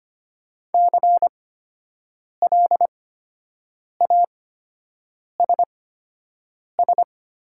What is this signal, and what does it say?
Morse code, heard as a keyed tone, about 25 wpm, spaced as for 7 wpm, 700 Hz. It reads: CLASS
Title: 8x Character Spacing - Top 500 Words - 25wpm